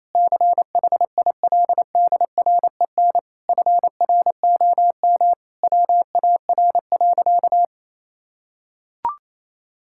Morse code, heard as a keyed tone, about 27 wpm, 700 Hz. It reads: CHILDREN FROM WAR.
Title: Sentences from Top 1000 Words - Review Part 6 - 28wpm